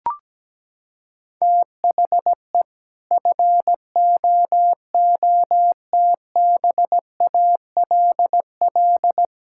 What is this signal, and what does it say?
Morse code, heard as a keyed tone, about 17 wpm, 700 Hz. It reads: THE FOOTBALL
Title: Sentences from Top 1600 Words 17wpm